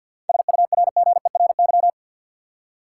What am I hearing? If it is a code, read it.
Morse code, heard as a keyed tone, about 52 wpm, 700 Hz. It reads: SURGERY
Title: Seven Letter Words - 50wpm